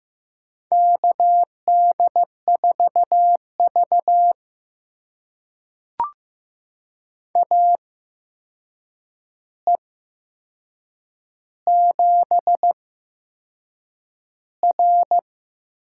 Morse code, heard as a keyed tone, about 16 wpm, spaced as for 6 wpm, 700 Hz. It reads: KD4V AE7R
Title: USA Call Signs - 4 Characters - 8x Character Spacing - 15wpm